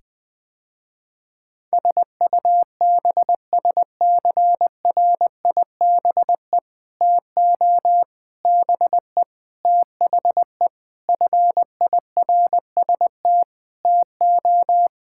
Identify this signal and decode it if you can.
Morse code, heard as a keyed tone, about 20 wpm, 700 Hz. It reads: SUBSCRIBE TO BE THE FIRST TO
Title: Sentences with 6-Word Phrases - Spoken Beginning - 20wpm